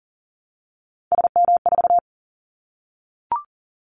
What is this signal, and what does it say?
Morse code, heard as a keyed tone, about 40 wpm, 700 Hz. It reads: SM4
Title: Callsign Prefixes for Top 25 DXCC - 40wpm